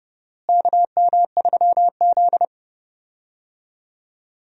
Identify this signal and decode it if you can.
Morse code, heard as a keyed tone, about 30 wpm, 700 Hz. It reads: KM3Z
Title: USA Call Signs - 4 Characters - Repeated 2x - 30wpm